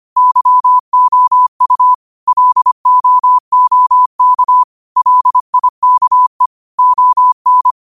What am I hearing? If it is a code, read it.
Morse code, heard as a keyed tone, about 25 wpm, 1000 Hz. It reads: YOU LOOK LIKE ON